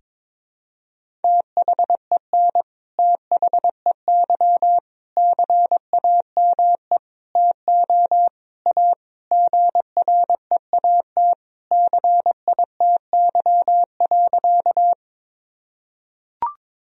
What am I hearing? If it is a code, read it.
Morse code, heard as a keyed tone, about 22 wpm, 700 Hz. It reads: THEN THEY CAME TO A GREAT CITY.
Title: Sentences from Top 1000 Words - Review Part 1 - 22wpm